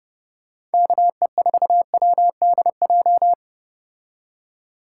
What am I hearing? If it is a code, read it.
Morse code, heard as a keyed tone, about 32 wpm, 700 Hz. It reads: KE4WDJ